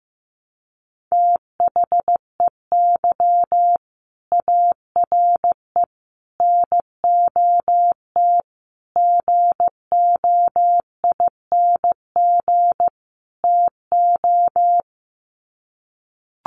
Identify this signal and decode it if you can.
Morse code, heard as a keyed tone, about 15 wpm, 700 Hz. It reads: THEY ARE NOT GOING TO
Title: Sets of 5 Words Encore2 15wpm